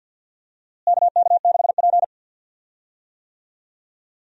Morse code, heard as a keyed tone, about 51 wpm, 700 Hz. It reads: KK6P
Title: USA Call Signs - 4 Characters - Repeated 2x - 50wpm